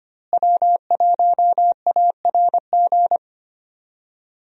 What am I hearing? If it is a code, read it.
Morse code, heard as a keyed tone, about 25 wpm, 700 Hz. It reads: W1ARG
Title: USA Call Signs - 5 Characters - Repeated 2x - 25wpm